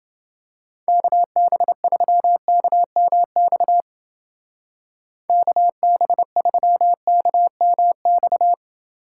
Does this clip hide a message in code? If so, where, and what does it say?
Yes: Morse code from 0.9 to 8.5 s, heard as a keyed tone, about 28 wpm, 700 Hz. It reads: KB3KMX KB3KMX